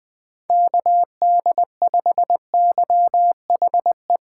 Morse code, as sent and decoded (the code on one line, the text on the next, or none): -.- -.. ..... -.-- .... .
KD5YHE